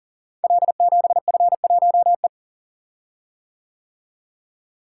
Morse code, heard as a keyed tone, about 42 wpm, 690 Hz. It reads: L7F1E
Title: Code Groups - 5 Letter:Numbers - 40wpm